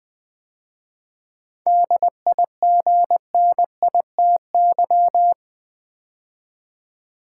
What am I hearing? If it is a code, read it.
Morse code, heard as a keyed tone, about 20 wpm, 690 Hz. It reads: DIGNITY
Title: 7 Letter Words - First 3 Letters Given - 20wpm